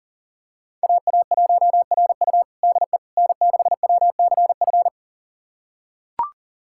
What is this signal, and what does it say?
Morse code, heard as a keyed tone, about 40 wpm, 700 Hz. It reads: AA1RU DE N6WCF
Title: USA Callsign DE Callsign - 5 Characters - 40wpm